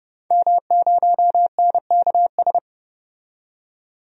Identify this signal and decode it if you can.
Morse code, heard as a keyed tone, about 31 wpm, 700 Hz. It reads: M0NKS
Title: DXCC-100 Call Signs - 5 Characters - 30wpm